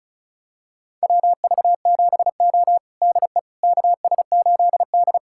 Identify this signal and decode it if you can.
Morse code, heard as a keyed tone, about 34 wpm, 700 Hz. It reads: WV7O DE KS8D
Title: USA Callsign DE Callsign - 4 Characters - 35wpm